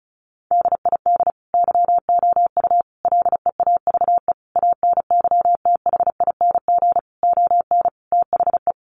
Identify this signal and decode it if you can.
Morse code, heard as a keyed tone, about 34 wpm, 700 Hz. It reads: DID YOU LEAVE ANYTHING ON THE
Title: Sentences from Top 1000 Words 35wpm